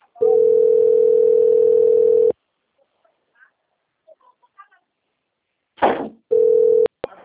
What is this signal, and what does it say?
Ringback tone (North America), 440 + 480 Hz.